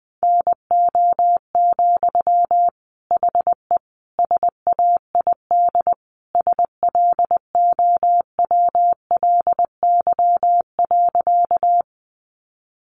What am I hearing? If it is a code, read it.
Morse code, heard as a keyed tone, about 20 wpm, 700 Hz. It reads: NO, HE SAID SLOWLY.
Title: Sentences from Top 1100 Words 20wpm